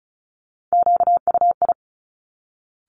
Morse code, heard as a keyed tone, about 36 wpm, 690 Hz. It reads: QUI
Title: Top Three Letter Combinations in Words 35wpm